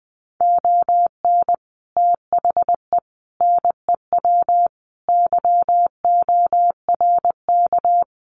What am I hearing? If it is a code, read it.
Morse code, heard as a keyed tone, about 20 wpm, 700 Hz. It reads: ON THE NEW YORK
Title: Sets of 5 Words Encore 20wpm